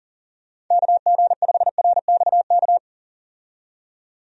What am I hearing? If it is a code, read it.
Morse code, heard as a keyed tone, about 37 wpm, 700 Hz. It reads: KG5RXK
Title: USA Call Signs - 6 Characters - 40wpm